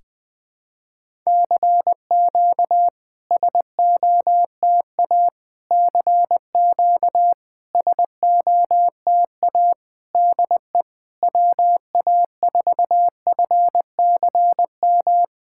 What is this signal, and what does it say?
Morse code, heard as a keyed tone, about 20 wpm, 700 Hz. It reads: CQ SOTA CQ SOTA DE WA4FCM
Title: SOTA QSO Practice 20wpm